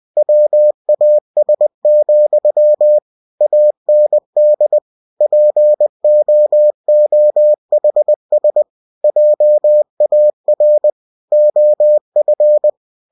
Morse code, heard as a keyed tone, about 20 wpm, 600 Hz. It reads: WAS, AND POOHS JAR OF
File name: cw-books / winnie-the-pooh / winnie-20wpm-page-0040